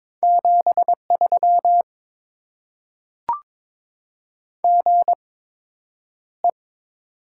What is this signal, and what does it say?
Morse code, heard as a keyed tone, about 22 wpm, spaced as for 9 wpm, 700 Hz. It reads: 73 GE
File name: Top 20 QSO Elements - 8x Character Spacing - 22wpm